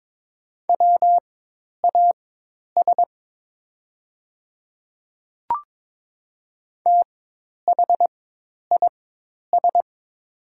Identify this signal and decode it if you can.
Morse code, heard as a keyed tone, about 22 wpm, spaced as for 10 wpm, 700 Hz. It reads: WAS THIS
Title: 4x Character Spacing - Top 100 Words - 22wpm